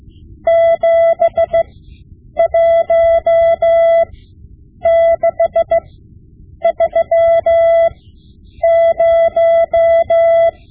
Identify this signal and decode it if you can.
Morse code, heard as a keyed tone, about 16 wpm, spaced as for 8 wpm, 670 Hz. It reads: 71630
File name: Morse Lessons / Morse Lessons / Tape-5B